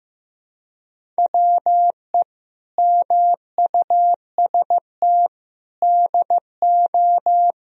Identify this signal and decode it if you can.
Morse code, heard as a keyed tone, about 16 wpm, 700 Hz. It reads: WE MUST DO
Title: Sentences from Top 1000 Words - Review Part 4 - 15wpm